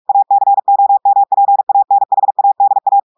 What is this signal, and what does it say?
Morse code, heard as a keyed tone, about 46 wpm, 800 Hz. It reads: ACOMPANHADA
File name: audios / portugais / portugais texte en clair 2 vitesse 45 mots minutes